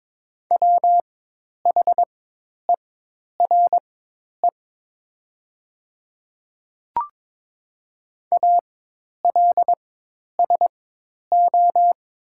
Morse code, heard as a keyed tone, about 22 wpm, spaced as for 10 wpm, 700 Hz. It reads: WHERE ALSO
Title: 4x Character Spacing - Top 200 Words - 22wpm